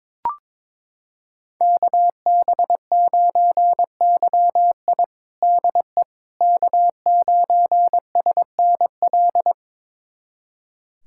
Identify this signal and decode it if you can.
Morse code, heard as a keyed tone, about 21 wpm, 700 Hz. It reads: KB9YI DE K9SNL
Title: USA Callsign DE Callsign - 5 Characters - 22wpm